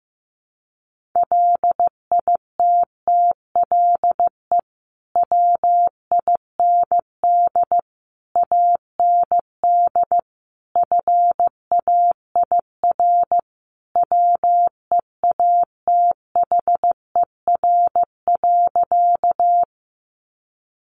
Morse code, heard as a keyed tone, about 15 wpm, 700 Hz. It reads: LITTLE WIND AND FAIR WEATHER.
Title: Sentences from Top 1300 Words 15wpm